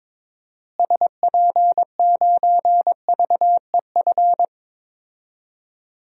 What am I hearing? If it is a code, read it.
Morse code, heard as a keyed tone, about 22 wpm, 700 Hz. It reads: SP9VEF
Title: DXCC-100 Call Signs - 6 Characters - Repeated 2x - 22wpm